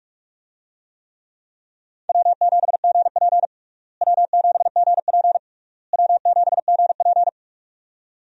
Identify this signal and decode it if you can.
Morse code, heard as a keyed tone, about 46 wpm, 700 Hz. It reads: W7GP W7GP W7GP
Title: USA Call Signs - 4 Characters - Repeated 3x - 45wpm